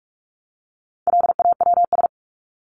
Morse code, heard as a keyed tone, about 44 wpm, 700 Hz. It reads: LAWS
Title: ES:S Suffix - Top 100 Words with Base Words 45wpm